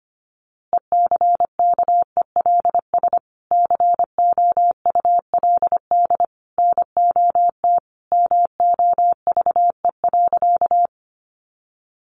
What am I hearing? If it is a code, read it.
Morse code, heard as a keyed tone, about 25 wpm, 700 Hz. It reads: ECKELS COULD NOT MOVE.
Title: A Sound Of Thunder 25wpm